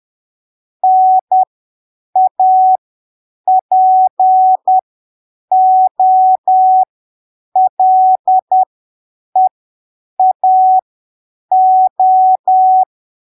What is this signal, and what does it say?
Morse code, heard as a keyed tone, about 10 wpm, spaced as for 7 wpm, 750 Hz. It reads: NAPOLEAO
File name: arquivos / texto / texto 26 10ppm